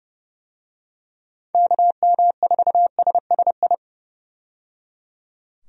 Morse code, heard as a keyed tone, about 30 wpm, 700 Hz. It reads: KM4SSI
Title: USA Call Signs - 6 Characters - 30wpm